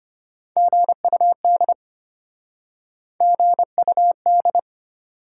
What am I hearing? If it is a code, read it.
Morse code, heard as a keyed tone, about 33 wpm, 700 Hz. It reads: GUD GUD